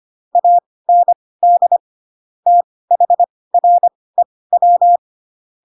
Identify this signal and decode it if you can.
Morse code, heard as a keyed tone, about 25 wpm, spaced as for 18 wpm, 700 Hz. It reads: AND THREW
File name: Pinocchio / Pinocchio ch4 12 25 ik1hge